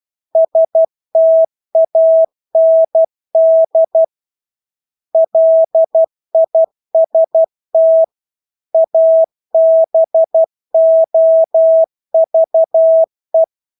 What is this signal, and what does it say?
Morse code, heard as a keyed tone, about 12 wpm, 650 Hz. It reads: STAND LIST ABOVE
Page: Engelske ord 12 wpm